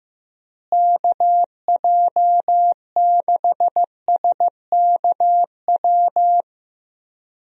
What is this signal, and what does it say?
Morse code, heard as a keyed tone, about 15 wpm, 700 Hz. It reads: KJ6SKW
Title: USA Call Signs - 6 Characters - Repeated 3x - 15wpm